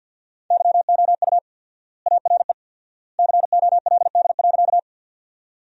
Morse code, heard as a keyed tone, about 50 wpm, 700 Hz. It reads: YOU ARE COLD.
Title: 4x Word-Spacing - Sentences from Top 400 Words - 50wpm